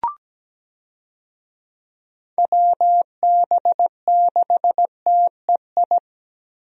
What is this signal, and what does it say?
Morse code, heard as a keyed tone, about 17 wpm, 700 Hz. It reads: WB6TEI
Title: USA Call Signs - 6 Characters - Repeated 2x - 17wpm